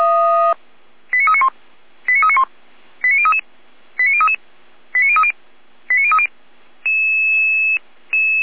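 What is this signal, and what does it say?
Five-tone selective call (ZVEI), heard as quick runs of five tones, tones of about 70 ms. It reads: 89391 89391 89033 89033 89030 89030
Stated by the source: ZVEI five-tone selective call 89058 received with an SDR, archived by Feuerwehr Orlamünde (F85360).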